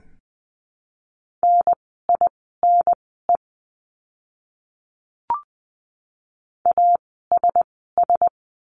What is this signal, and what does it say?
Morse code, heard as a keyed tone, about 20 wpm, spaced as for 14 wpm, 700 Hz. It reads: NINE ASS
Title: 054 Lesson 10f - Letter r - Words 20wpm